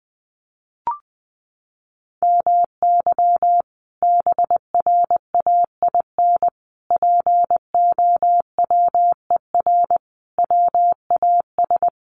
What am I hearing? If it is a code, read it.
Morse code, heard as a keyed tone, about 20 wpm, 700 Hz. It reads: MY BRAIN POWER WAS
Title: Sentences from Top 1400 Words 20wpm